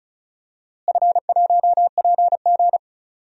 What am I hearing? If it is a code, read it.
Morse code, heard as a keyed tone, about 34 wpm, 700 Hz. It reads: F1PG
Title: DXCC-100 Call Signs - 4 Characters - Repeated 2x - 35wpm